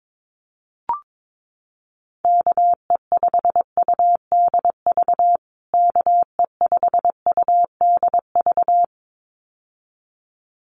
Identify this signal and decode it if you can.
Morse code, heard as a keyed tone, about 22 wpm, 700 Hz. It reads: KE5UDV KE5UDV
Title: USA Call Signs Repeated 2x 22wpm